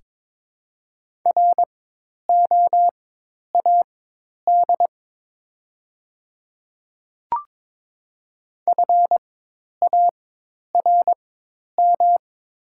Morse code, heard as a keyed tone, about 22 wpm, spaced as for 10 wpm, 700 Hz. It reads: ROAD FARM
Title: 4x Character Spacing - Top 500 Words - 22wpm